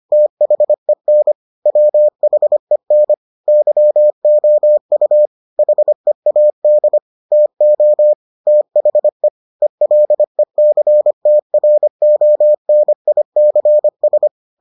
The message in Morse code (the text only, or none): THEN WHEN YOU HEAD TO THE ELECTRONICS